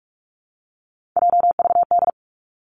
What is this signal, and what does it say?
Morse code, heard as a keyed tone, about 46 wpm, 700 Hz. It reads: JVD
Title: Three Challenging Letter-Number Combinations 45wpm